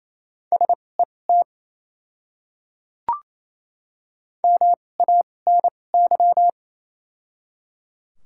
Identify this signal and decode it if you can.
Morse code, heard as a keyed tone, about 28 wpm, spaced as for 20 wpm, 700 Hz. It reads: SET MANY
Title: 2x Character Spacing - Top 200 Words - 28wpm